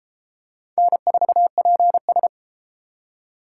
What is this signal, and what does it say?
Morse code, heard as a keyed tone, about 28 wpm, 700 Hz. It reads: N4PS